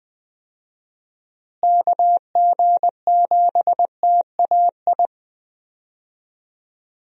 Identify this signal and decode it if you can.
Morse code, heard as a keyed tone, about 20 wpm, 700 Hz. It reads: KG7TAI
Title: USA Call Signs - 6 Characters - Repeated 3x - 20wpm